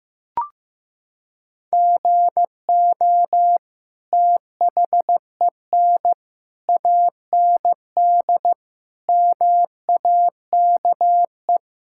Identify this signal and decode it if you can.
Morse code, heard as a keyed tone, about 15 wpm, 700 Hz. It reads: GO THEN AND MAKE
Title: Sentences from Top 1000 Words - Review Part 5 - 15wpm